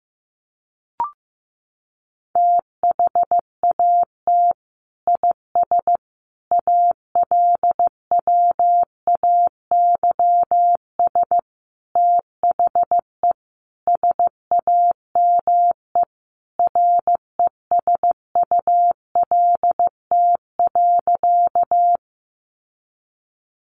Morse code, heard as a keyed tone, about 15 wpm, 700 Hz. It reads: THAT IS ALWAYS THE SAME RESULT.
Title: Sentences from Top 700 Words 15wpm